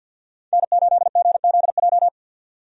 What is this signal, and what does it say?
Morse code, heard as a keyed tone, about 50 wpm, 700 Hz. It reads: N8GZJ
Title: USA Call Signs - 5 Characters - 50wpm